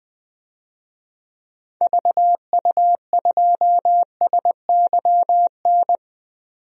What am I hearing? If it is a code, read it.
Morse code, heard as a keyed tone, about 20 wpm, 700 Hz. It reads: VU2SYN
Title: DXCC-100 Call Signs - 6 Characters - 20wpm